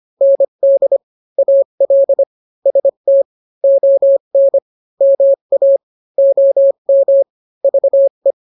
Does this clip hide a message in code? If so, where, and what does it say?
Yes: Morse code from 0.2 to 8.3 s, heard as a keyed tone, about 25 wpm, 550 Hz. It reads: ND AL ST ON MA OM VE